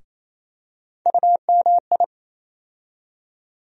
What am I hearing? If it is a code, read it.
Morse code, heard as a keyed tone, about 28 wpm, 700 Hz. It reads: UMI
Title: USA Call Sign Postfixes - 3 Letters 28wpm